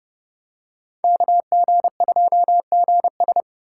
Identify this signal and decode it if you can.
Morse code, heard as a keyed tone, about 28 wpm, 700 Hz. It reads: KG2GS